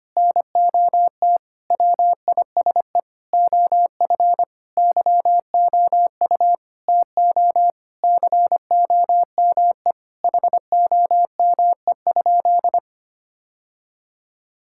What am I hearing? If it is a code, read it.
Morse code, heard as a keyed tone, about 25 wpm, 700 Hz. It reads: NOT WISE OF YOU TO COME HOME?
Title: Sentences from Top 1600 Words 25wpm